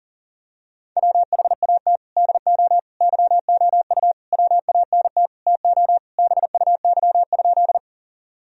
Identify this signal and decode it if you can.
Morse code, heard as a keyed tone, about 40 wpm, 700 Hz. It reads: WHAT DO YOU WANT TO BUY?
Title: Sentences from Top 1000 Words - Review Part 4 - 40wpm